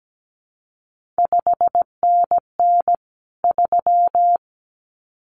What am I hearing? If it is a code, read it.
Morse code, heard as a keyed tone, about 17 wpm, 700 Hz. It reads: 5NN 3
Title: CQ WW DX Contest Exchange Encore - 17wpm